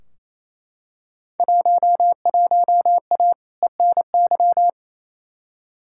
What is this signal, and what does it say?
Morse code, heard as a keyed tone, about 28 wpm, 700 Hz. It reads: 11A ENY
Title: Field Day Exchanges Encore - 28wpm